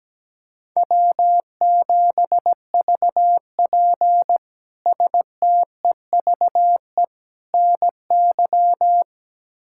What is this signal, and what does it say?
Morse code, heard as a keyed tone, about 17 wpm, 700 Hz. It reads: W7VP STEVE NY
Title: K1USN SST QSO Practice - 17wpm